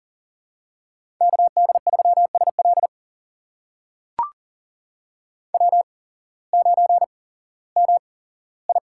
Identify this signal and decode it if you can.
Morse code, heard as a keyed tone, about 42 wpm, spaced as for 19 wpm, 700 Hz. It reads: KD3SL W9MI
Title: USA Call Signs - 5 Characters - 8x Character Spacing - 40wpm